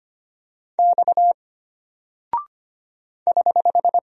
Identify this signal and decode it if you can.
Morse code, heard as a keyed tone, about 25 wpm, 700 Hz. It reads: X <HH>
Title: Single Character (Letter:Number:Punctuation:Prosign) Alt 25wpm